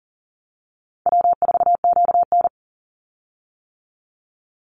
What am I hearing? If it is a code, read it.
Morse code, heard as a keyed tone, about 40 wpm, 700 Hz. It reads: W4QN
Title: Call Signs - 40wpm